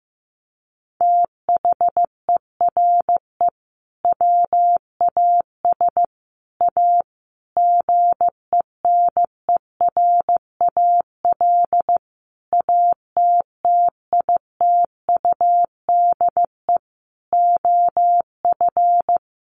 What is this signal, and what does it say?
Morse code, heard as a keyed tone, about 15 wpm, 700 Hz. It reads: THERE WAS A GENERAL ATTITUDE OF
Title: Sentences from Top 1300 Words 15wpm